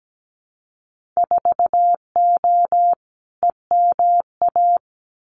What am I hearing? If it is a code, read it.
Morse code, heard as a keyed tone, about 17 wpm, 700 Hz. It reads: 4O EMA
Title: Winter Field Day Exchanges 17wpm